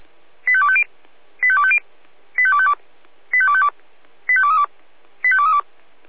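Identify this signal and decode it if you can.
Five-tone selective call (ZVEI), heard as quick runs of five tones, tones of about 70 ms. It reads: 85250 85250 85252 85252 85232 85232
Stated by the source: ZVEI five-tone call 85894 received on SDR.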